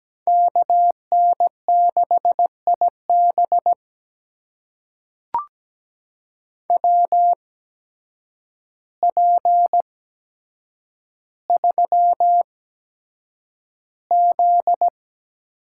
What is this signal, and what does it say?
Morse code, heard as a keyed tone, about 17 wpm, spaced as for 8 wpm, 700 Hz. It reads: KN6IB WP3Z
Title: USA Call Signs - 5 Characters - 8x Character Spacing - 17wpm